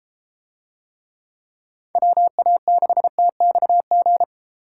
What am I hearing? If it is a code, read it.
Morse code, heard as a keyed tone, about 30 wpm, 700 Hz. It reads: WA6TXG